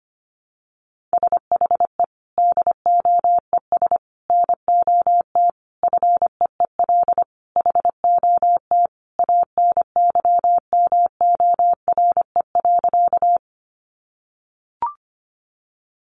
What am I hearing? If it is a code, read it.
Morse code, heard as a keyed tone, about 25 wpm, 700 Hz. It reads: SHE DOES NOT FEEL HOT ANYMORE.
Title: Additional Easy Phrases 25wpm